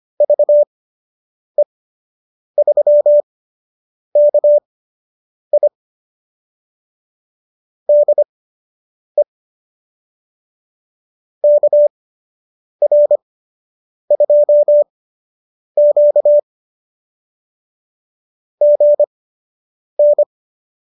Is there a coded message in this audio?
Yes: Morse code from 0.2 to 20.2 s, heard as a keyed tone, about 25 wpm, spaced as for 8 wpm, 590 Hz. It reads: VE3KI DE KR2Q GN